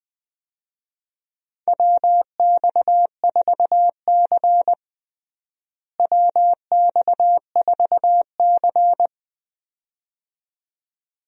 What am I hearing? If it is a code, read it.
Morse code, heard as a keyed tone, about 20 wpm, 700 Hz. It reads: WX4C WX4C
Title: USA Call Signs - 4 Characters - Repeated 2x - 20wpm